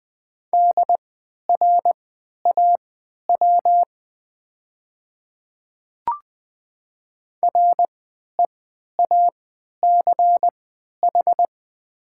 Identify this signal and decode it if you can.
Morse code, heard as a keyed tone, about 20 wpm, spaced as for 11 wpm, 700 Hz. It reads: DRAW REACH
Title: 3x Character Spacing - Top 500 Words - 20wpm